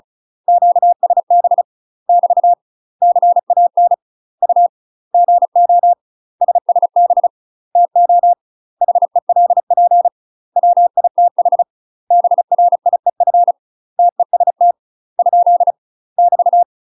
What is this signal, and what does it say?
Morse code, heard as a keyed tone, about 34 wpm, 700 Hz. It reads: QSB <BT> CAN U GO SSB TO HELP WITH BRIEF TEST ? <BT>
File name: UK MASTER QSOs TRAINING / 35WPM / UK MASTER QSOs 0020